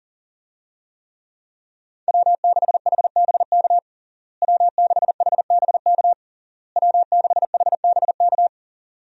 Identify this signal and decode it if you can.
Morse code, heard as a keyed tone, about 40 wpm, 700 Hz. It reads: W6HBK W6HBK W6HBK
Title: USA Call Signs - 5 Characters - Repeated 3x - 40wpm